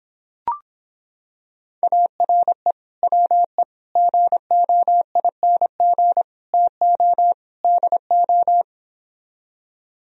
Morse code, heard as a keyed tone, about 26 wpm, 700 Hz. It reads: ARE WE GOING TO DO